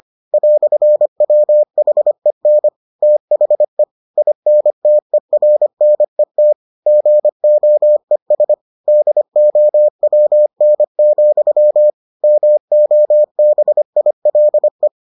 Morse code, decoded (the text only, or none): "WHEN THE INTERNET GOES DOWN, MOBILE